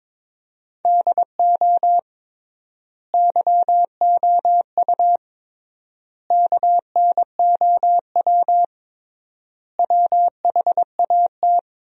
Morse code, heard as a keyed tone, about 22 wpm, 700 Hz. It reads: DO YOU KNOW WHAT
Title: 3x Word-Spacing - Sentences from Top 100 Words - 22wpm